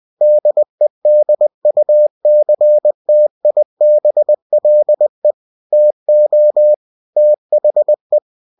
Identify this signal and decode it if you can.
Morse code, heard as a keyed tone, about 20 wpm, 600 Hz. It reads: DEDUCTIBLE TO THE